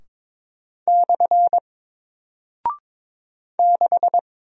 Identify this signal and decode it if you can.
Morse code, heard as a keyed tone, about 22 wpm, 700 Hz. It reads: / 6
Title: Single Character (Letter:Number:Punctuation:Prosign) Alt 22wpm